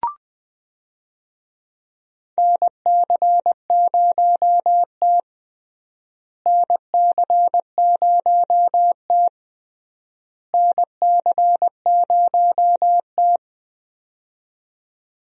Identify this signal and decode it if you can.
Morse code, heard as a keyed tone, about 20 wpm, 700 Hz. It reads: NC0T NC0T NC0T
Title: USA Call Signs - 4 Characters - Repeated 3x - 20wpm